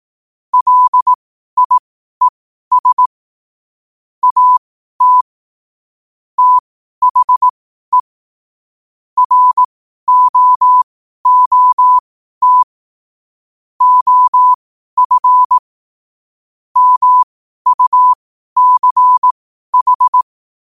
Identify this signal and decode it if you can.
Morse code, heard as a keyed tone, about 18 wpm, spaced as for 12 wpm, 1000 Hz. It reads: LIES AT THE ROOT OF MUCH
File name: qotdfiles / 12wpm